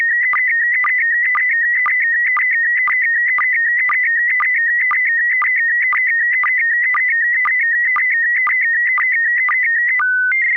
SSTV (slow-scan television), PD 120.
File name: SELFIESAT / sstv-pd120-selfiesat